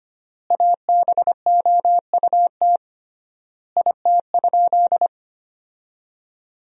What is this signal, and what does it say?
Morse code, heard as a keyed tone, about 25 wpm, 700 Hz. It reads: ABOUT IT?
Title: 3x Word-Spacing - Sentences from Top 100 Words - 25wpm